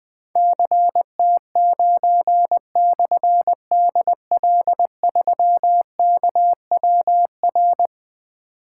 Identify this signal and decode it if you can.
Morse code, heard as a keyed tone, about 20 wpm, 700 Hz. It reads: CT9/DL3KWR
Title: Difficult Call Signs 20wpm